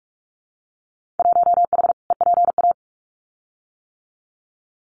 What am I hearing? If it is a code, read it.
Morse code, heard as a keyed tone, about 46 wpm, 700 Hz. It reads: 1H EPA
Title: Winter Field Day Exchanges 45wpm